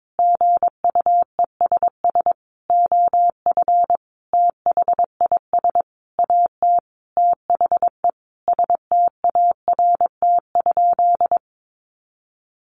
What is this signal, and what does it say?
Morse code, heard as a keyed tone, about 22 wpm, 700 Hz. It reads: GUESS OF THIS AT THE START?
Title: Sentences from Top 1500 Words 22wpm